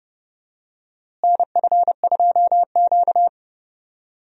Morse code, heard as a keyed tone, about 30 wpm, 700 Hz. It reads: NF2Q
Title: USA Call Signs - 4 Characters - Repeated 3x - 30wpm